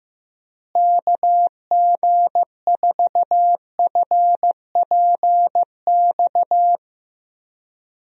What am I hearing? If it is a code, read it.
Morse code, heard as a keyed tone, about 15 wpm, 700 Hz. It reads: KG4FPX
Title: USA Call Signs - 6 Characters - Repeated 3x - 15wpm